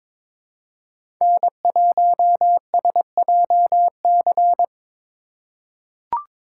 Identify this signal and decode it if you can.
Morse code, heard as a keyed tone, about 22 wpm, 700 Hz. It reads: N1SJC
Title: USA Call Signs - 5 Characters - 2x Character Spacing - 22wpm